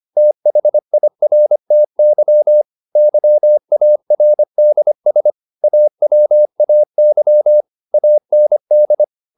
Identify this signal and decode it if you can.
Morse code, heard as a keyed tone, about 25 wpm, 600 Hz. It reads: THIRTY YARDS AWAY AND